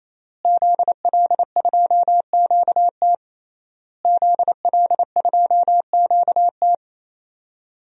Morse code, heard as a keyed tone, about 27 wpm, 700 Hz. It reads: ZL2QT ZL2QT
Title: DXCC-100 Call Signs - 5 Characters - Repeated 3x - 28wpm